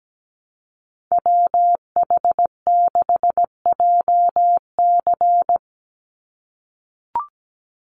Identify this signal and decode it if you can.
Morse code, heard as a keyed tone, about 17 wpm, 700 Hz. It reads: WH6JC
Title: USA Call Signs 17wpm